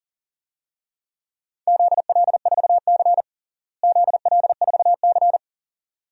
Morse code, heard as a keyed tone, about 37 wpm, 700 Hz. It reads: ZL4C ZL4C
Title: DXCC-100 Call Signs - 4 Characters - Repeated 2x - 40wpm